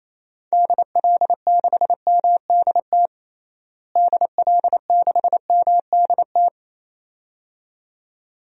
Morse code, heard as a keyed tone, about 28 wpm, 700 Hz. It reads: DL6MDT DL6MDT
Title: DXCC-100 Call Signs - 6 Characters - Repeated 2x - 28wpm